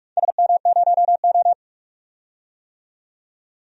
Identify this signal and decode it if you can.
Morse code, heard as a keyed tone, about 46 wpm, 700 Hz. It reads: SM0O
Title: DXCC-100 Call Signs - 4 Characters - 45wpm